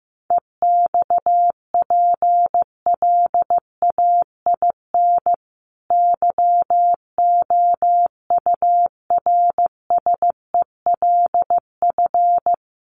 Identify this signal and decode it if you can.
Morse code, heard as a keyed tone, about 15 wpm, 700 Hz. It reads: EXPLAIN YOURSELF
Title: Sentences from Top 1300 Words 15wpm